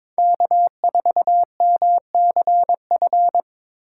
Morse code, heard as a keyed tone, about 22 wpm, 700 Hz. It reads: K4MCF
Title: USA Call Signs - 5 Characters - 3x Character Spacing - 22wpm